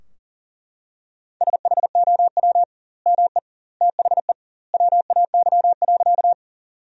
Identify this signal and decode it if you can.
Morse code, heard as a keyed tone, about 40 wpm, 700 Hz. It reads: SHOW ME THE WAY.
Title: 2x Word-Spacing - Sentences from Top 200 Words - 40wpm